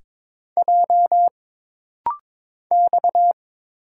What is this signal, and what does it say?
Morse code, heard as a keyed tone, about 22 wpm, 700 Hz. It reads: J X
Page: This rapid-fire timing is geared to push your ICR (Instant Character Recognition) skills.